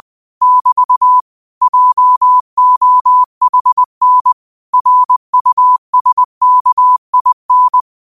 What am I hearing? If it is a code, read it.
Morse code, heard as a keyed tone, about 20 wpm, 1000 Hz. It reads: <BT> JOHN RUSKIN
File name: qotdfiles / 20wpm